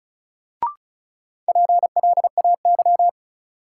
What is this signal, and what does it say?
Morse code, heard as a keyed tone, about 36 wpm, 700 Hz. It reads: PLAY
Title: Sentences from Top 200 Words - New Word Drill - 35wpm